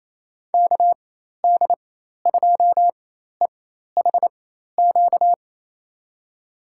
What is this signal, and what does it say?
Morse code, heard as a keyed tone, about 28 wpm, spaced as for 13 wpm, 700 Hz. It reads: KD2EHQ
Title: USA Call Signs - 6 Characters - 4x Character Spacing - 28wpm